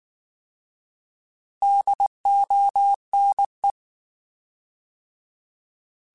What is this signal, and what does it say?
Morse code, heard as a keyed tone, about 19 wpm, 800 Hz. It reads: DONE
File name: Alphabetical / Alphabetical mp3s / Ds @18